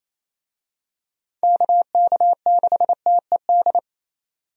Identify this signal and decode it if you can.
Morse code, heard as a keyed tone, about 28 wpm, 700 Hz. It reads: KK6TED
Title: USA Call Signs - 6 Characters - 28wpm